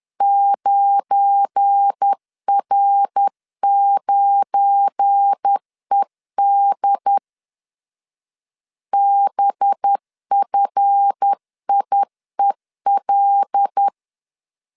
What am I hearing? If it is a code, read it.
Morse code, heard as a keyed tone, about 10 wpm, 800 Hz. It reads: 9R9ED BFIEL
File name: LEZIONI / 7 GF9Z / 7 GF9 10wpm 2747